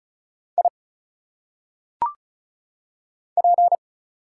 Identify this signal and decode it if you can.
Morse code, heard as a keyed tone, about 35 wpm, 700 Hz. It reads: I P
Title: Single Letters - Phonetic Pronunciation 35wpm